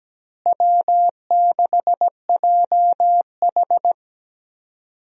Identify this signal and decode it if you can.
Morse code, heard as a keyed tone, about 17 wpm, 700 Hz. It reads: W6JH
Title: USA Call Signs - 4 Characters - Repeated 3x - 17wpm